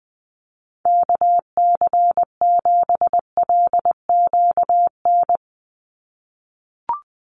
Morse code, heard as a keyed tone, about 20 wpm, 700 Hz. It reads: KC7LQN
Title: USA Call Signs 20wpm